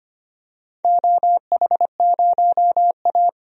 Morse code, heard as a keyed tone, about 25 wpm, 700 Hz. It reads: OH0A
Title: DXCC-100 Call Signs - 4 Characters - 25wpm